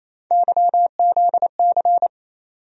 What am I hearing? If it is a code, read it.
Morse code, heard as a keyed tone, about 27 wpm, 700 Hz. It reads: YZC
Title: USA Call Sign Postfixes - 3 Letters 28wpm